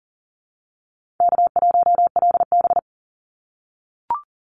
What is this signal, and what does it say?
Morse code, heard as a keyed tone, about 40 wpm, 700 Hz. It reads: K1LB
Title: NA SOTA Chaser Callsigns 40wpm